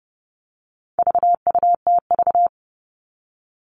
Morse code, heard as a keyed tone, about 30 wpm, 700 Hz. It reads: VUTV
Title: All Cut Numbers - 4-Digit 30wpm